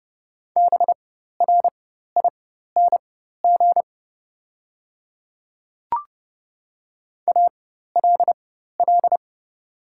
Morse code, heard as a keyed tone, about 30 wpm, spaced as for 14 wpm, 700 Hz. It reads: BRING ALL
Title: 4x Character Spacing - Top 500 Words - 30wpm